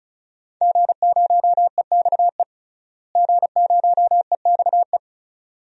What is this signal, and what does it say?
Morse code, heard as a keyed tone, about 36 wpm, 700 Hz. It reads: G0EXE G0EXE
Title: DXCC-100 Call Signs - 5 Characters - Repeated 2x - 35wpm